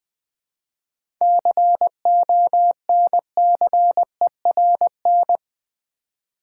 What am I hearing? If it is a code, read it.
Morse code, heard as a keyed tone, about 20 wpm, 700 Hz. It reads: CONCERN
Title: Seven Letter Words - 20wpm